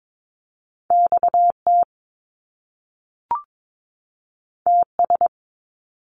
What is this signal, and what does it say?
Morse code, heard as a keyed tone, about 22 wpm, 700 Hz. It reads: XT TS
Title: Top Two Letter Combinations in Words 22wpm